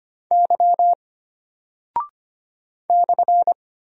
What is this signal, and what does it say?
Morse code, heard as a keyed tone, about 25 wpm, 700 Hz. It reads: Y /
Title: Single Character (Letter:Number:Punctuation:Prosign) Alt 25wpm